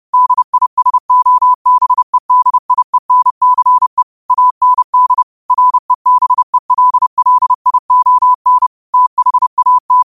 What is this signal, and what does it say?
Morse code, heard as a keyed tone, about 30 wpm, 1000 Hz. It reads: DISOBEDIENCE AND REBELLION THAT
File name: qotdfiles / 30wpm